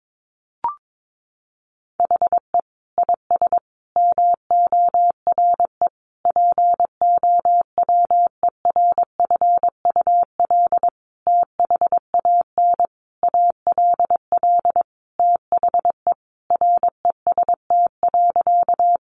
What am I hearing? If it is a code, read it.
Morse code, heard as a keyed tone, about 22 wpm, 700 Hz. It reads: HE IS MORE POWERFUL THAN ALL THE REST.
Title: Sentences from Top 1300 Words 22wpm